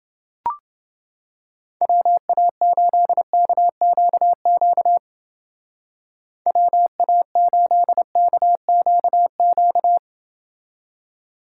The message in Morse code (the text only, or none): WA8KQQ WA8KQQ